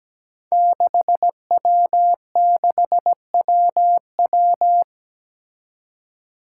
Morse code, heard as a keyed tone, about 17 wpm, 700 Hz. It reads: 6W6WW
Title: Five Challenging Letter-Number Combinations 17wpm